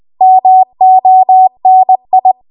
Morse code, heard as a keyed tone, about 20 wpm, 750 Hz. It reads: MONI
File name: Archive / 20 WPM / 20WPM